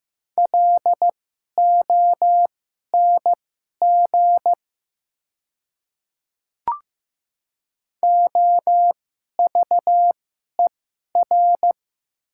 Morse code, heard as a keyed tone, about 15 wpm, spaced as for 11 wpm, 690 Hz. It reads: LONG OVER
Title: 2x Character Spacing - Top 100 Words - 15wpm